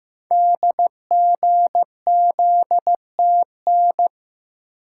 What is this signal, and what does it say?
Morse code, heard as a keyed tone, about 15 wpm, 700 Hz. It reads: DGZTN
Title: Code Groups - 5 Letter:Numbers - 15wpm